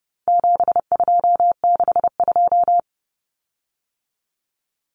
Morse code, heard as a keyed tone, about 30 wpm, 700 Hz. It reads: 7262
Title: Common Cut Numbers - 4-Digit 30wpm